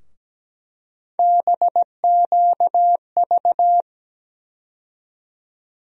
Morse code, heard as a keyed tone, about 17 wpm, 700 Hz. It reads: BQV
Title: USA Call Sign Postfixes - 3 Letters 17wpm